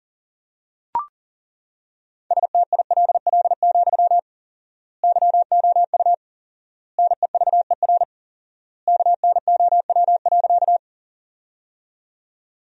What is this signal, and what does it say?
Morse code, heard as a keyed tone, about 38 wpm, 700 Hz. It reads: STILL, YOU NEVER KNOW.
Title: 4x Word-Spacing - Sentences from Top 300 Words - 40wpm